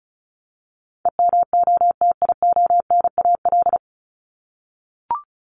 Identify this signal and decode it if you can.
Morse code, heard as a keyed tone, about 34 wpm, 700 Hz. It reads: EMOTIONAL
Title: Top 501-750 Words 35wpm